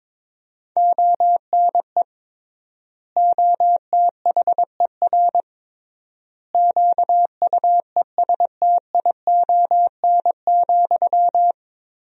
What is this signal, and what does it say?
Morse code, heard as a keyed tone, about 22 wpm, 700 Hz. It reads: ONE OTHER QUESTION,
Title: 3x Word-Spacing - Sentences from Top 400 Words - 22wpm